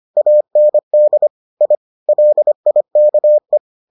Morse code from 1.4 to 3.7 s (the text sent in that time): I LIKE